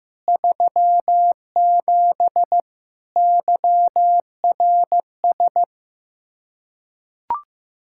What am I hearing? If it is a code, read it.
Morse code, heard as a keyed tone, about 15 wpm, 700 Hz. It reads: 37 YRS
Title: Practice QSO - Exchange 3 - More Info - 15wpm